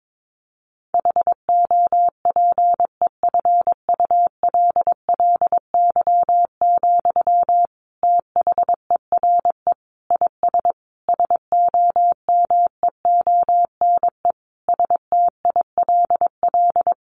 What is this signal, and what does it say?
Morse code, heard as a keyed tone, about 22 wpm, 700 Hz. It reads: HOPEFULLY, THERE IS SOMEONE STILL
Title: Sentences from Top 1500 Words 22wpm